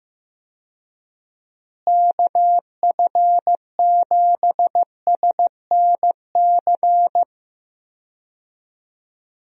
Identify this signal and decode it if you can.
Morse code, heard as a keyed tone, about 15 wpm, 700 Hz. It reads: KF7SNC
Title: USA Call Signs - 6 Characters - Repeated 2x - 15wpm